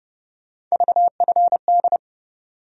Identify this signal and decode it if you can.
Morse code, heard as a keyed tone, about 30 wpm, 700 Hz. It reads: VFD
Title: USA Call Sign Postfixes - 3 Letters 30wpm